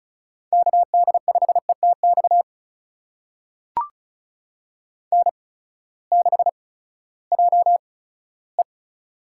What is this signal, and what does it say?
Morse code, heard as a keyed tone, about 34 wpm, spaced as for 17 wpm, 700 Hz. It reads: KD5ETX N6JE
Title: USA Call Signs - 8x Character Spacing - 35wpm